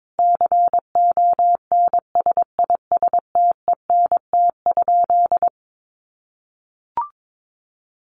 Morse code, heard as a keyed tone, about 22 wpm, 700 Hz. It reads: CONSISTENT?
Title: Sentences from Top 1500 Words 22wpm